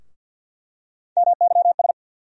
Morse code, heard as a keyed tone, about 47 wpm, 700 Hz. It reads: MYS
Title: USA Call Sign Postfixes - 3 Letters 50wpm